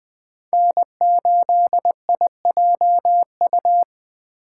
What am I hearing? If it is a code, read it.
Morse code, heard as a keyed tone, about 19 wpm, 700 Hz. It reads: N8IJU